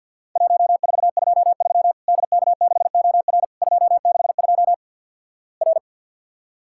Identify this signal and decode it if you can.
Morse code, heard as a keyed tone, about 50 wpm, 700 Hz. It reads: 1423 DK6OR 262 R
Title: Worked All Europe DX Contest - QTC Exchange with Repeats - 50wpm